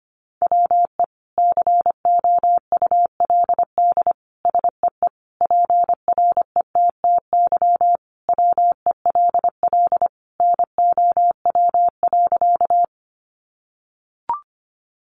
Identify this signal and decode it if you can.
Morse code, heard as a keyed tone, about 25 wpm, 700 Hz. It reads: WE COULD SEE PRETTY WELL NOW.
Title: Sentences from Top 1000 Words 25wpm